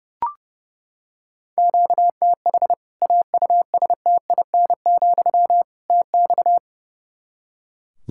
Morse code, heard as a keyed tone, about 30 wpm, 700 Hz. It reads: QTH AUSTIN, TX
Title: Practice QSO - Exchange 1 - Beginning - 30wpm